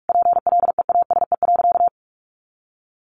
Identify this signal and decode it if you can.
Morse code, heard as a keyed tone, about 44 wpm, 700 Hz. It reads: PLEASE.
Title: Sentences from Top 1500 Words 45wpm